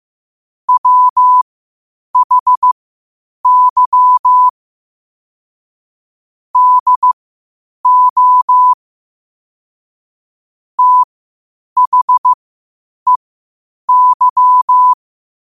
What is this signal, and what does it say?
Morse code, heard as a keyed tone, about 15 wpm, spaced as for 8 wpm, 1000 Hz. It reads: WHY DO THEY
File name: qotdfiles / 8wpm